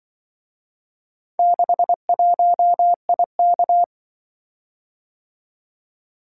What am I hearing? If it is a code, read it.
Morse code, heard as a keyed tone, about 24 wpm, 700 Hz. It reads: 61IK
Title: Speed-Racing - Four Letter-Number Combinations - 20wpm